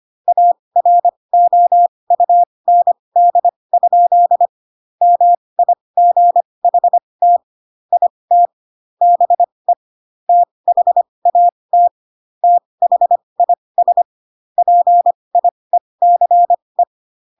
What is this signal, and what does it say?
Morse code, heard as a keyed tone, about 25 wpm, spaced as for 20 wpm, 700 Hz. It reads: AROUND? MIGHT IT BE THAT THIS PIECE
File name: Pinocchio / Pinocchio ch1 15 25 ik1hge